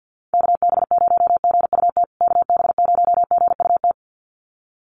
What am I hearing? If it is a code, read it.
Morse code, heard as a keyed tone, about 50 wpm, 700 Hz. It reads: KB0GUT KB0GUT
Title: USA Call Signs Repeated 2x 50wpm